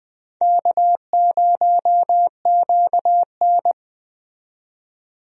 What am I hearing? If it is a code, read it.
Morse code, heard as a keyed tone, about 20 wpm, 700 Hz. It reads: K0QN
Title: USA Call Signs - 4 Characters - 20wpm